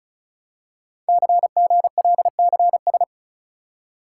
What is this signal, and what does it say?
Morse code, heard as a keyed tone, about 36 wpm, 700 Hz. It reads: CGLCS
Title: Code Groups - 5 Letters - 35wpm